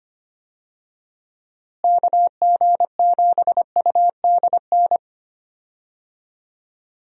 Morse code, heard as a keyed tone, about 25 wpm, 700 Hz. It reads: KG7UDN
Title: USA Call Signs - 6 Characters - Repeated 3x - 25wpm